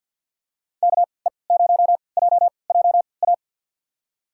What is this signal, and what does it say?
Morse code, heard as a keyed tone, about 52 wpm, spaced as for 29 wpm, 700 Hz. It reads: KE0JJA
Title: USA Call Signs - 6 Characters - 3x Character Spacing - 50wpm